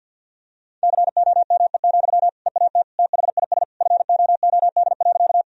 Morse code, heard as a keyed tone, about 50 wpm, 700 Hz. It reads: COME, EAT THIS FOOD.
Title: Sentences from Top 1000 Words - Review Part 1 - 50wpm